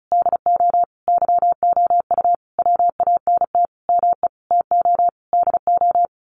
Morse code, heard as a keyed tone, about 34 wpm, 700 Hz. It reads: DO YOU WANT ME TO DO
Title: A Sound Of Thunder 35wpm